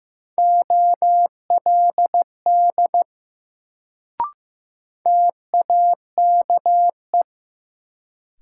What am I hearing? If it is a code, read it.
Morse code, heard as a keyed tone, about 15 wpm, 700 Hz. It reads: OLD TAKE
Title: Sentences from Top 200 Words - New Word Drill - 15wpm